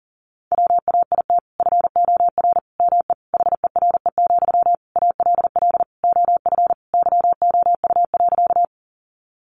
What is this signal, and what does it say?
Morse code, heard as a keyed tone, about 40 wpm, 700 Hz. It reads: WAIT FOR ME HERE, ALL OF YOU.
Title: Sentences from Top 500 Words 40wpm